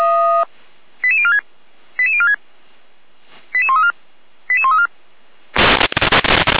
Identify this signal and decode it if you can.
Five-tone selective call (ZVEI), heard as quick runs of five tones, tones of about 70 ms. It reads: 88046 88046 88134 88134